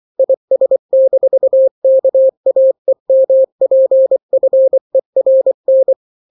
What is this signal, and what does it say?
Morse code, heard as a keyed tone, about 24 wpm, 530 Hz. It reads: IS-KAEMPFERN
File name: daily / 23wpm